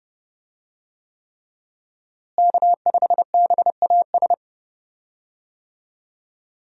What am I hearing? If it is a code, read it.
Morse code, heard as a keyed tone, about 30 wpm, 700 Hz. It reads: K5BAS
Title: USA Call Signs - 5 Characters - Repeated 3x - 30wpm